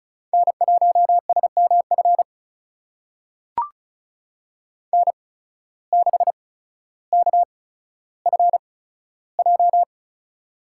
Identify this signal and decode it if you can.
Morse code, heard as a keyed tone, about 36 wpm, spaced as for 15 wpm, 700 Hz. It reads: N1SMF N6KFJ
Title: USA Call Signs - 5 Characters - 8x Character Spacing - 35wpm